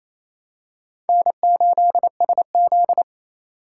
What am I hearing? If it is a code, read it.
Morse code, heard as a keyed tone, about 29 wpm, 700 Hz. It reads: N8SZ
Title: USA Call Signs - 4 Characters - 2x Character Spacing - 28wpm